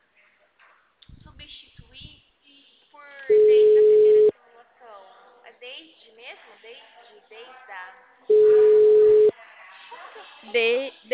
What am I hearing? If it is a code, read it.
Ringback tone (Europe, 425 Hz), with speech.